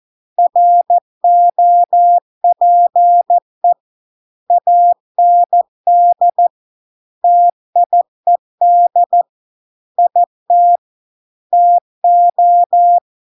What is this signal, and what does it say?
Morse code, heard as a keyed tone, about 14 wpm, 700 Hz. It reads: ROPE AND TIED IT TO